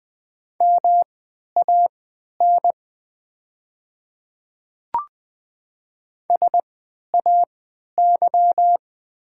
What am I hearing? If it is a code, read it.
Morse code, heard as a keyed tone, about 20 wpm, spaced as for 11 wpm, 700 Hz. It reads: MAN SAY
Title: 3x Character Spacing - Top 200 Words - 20wpm